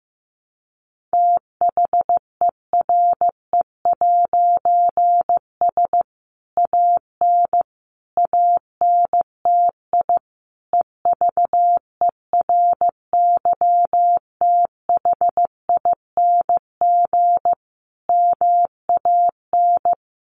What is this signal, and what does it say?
Morse code, heard as a keyed tone, about 15 wpm, 700 Hz. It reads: THERE'S AN ANTI EVERYTHING MAN
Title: A Sound Of Thunder 15wpm